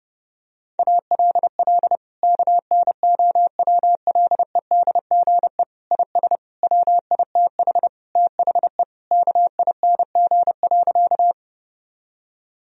Morse code, heard as a keyed tone, about 32 wpm, 700 Hz. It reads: ALL KNOWLEDGE IS WITH THE KING.
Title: Sentences from Top 1000 Words - Review Part 3 - 30wpm